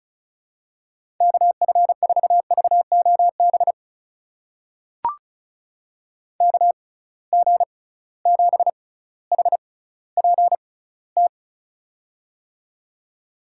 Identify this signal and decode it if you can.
Morse code, heard as a keyed tone, about 36 wpm, spaced as for 18 wpm, 700 Hz. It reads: KF4VOB KG7HPT
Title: USA Call Signs - 6 Characters - 6x Character Spacing - 35wpm